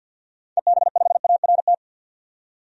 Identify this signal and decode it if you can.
Morse code, heard as a keyed tone, about 45 wpm, 700 Hz. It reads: EB5ART
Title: DXCC-100 Call Signs - 6 Characters - 50wpm